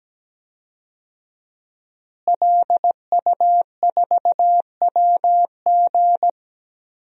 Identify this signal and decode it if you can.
Morse code, heard as a keyed tone, about 17 wpm, 700 Hz. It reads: LU4WG
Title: DXCC-100 Call Signs - 5 Characters - Repeated 3x - 17wpm